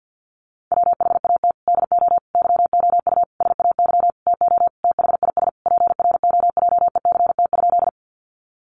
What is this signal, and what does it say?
Morse code, heard as a keyed tone, about 50 wpm, 700 Hz. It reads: WHAT DO YOU SAY TO THIS PROJECT?
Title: Sentences from Top 1400 Words 50wpm